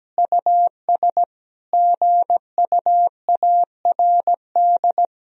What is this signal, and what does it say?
Morse code, heard as a keyed tone, about 17 wpm, 700 Hz. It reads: US GUARD
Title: Sentences from Top 1000 Words - Review Part 4 - 17wpm